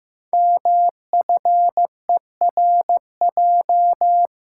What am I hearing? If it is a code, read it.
Morse code, heard as a keyed tone, about 15 wpm, 700 Hz. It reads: MFERJ
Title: Code Groups - 5 Letters - 15wpm